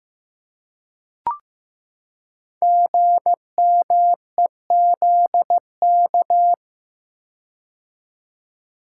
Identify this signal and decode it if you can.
Morse code, heard as a keyed tone, about 15 wpm, 700 Hz. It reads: GMEZK
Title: Code Groups - 5 Letter:Numbers - 15wpm